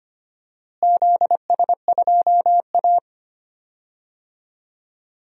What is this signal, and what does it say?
Morse code, heard as a keyed tone, about 25 wpm, 700 Hz. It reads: ZS2A
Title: DXCC-100 Call Signs - 4 Characters - Repeated 2x - 25wpm